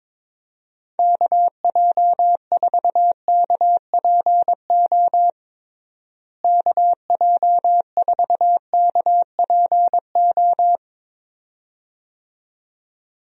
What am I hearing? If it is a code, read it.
Morse code, heard as a keyed tone, about 22 wpm, 700 Hz. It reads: KJ4KPO KJ4KPO
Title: USA Call Signs - 6 Characters - Repeated 2x - 22wpm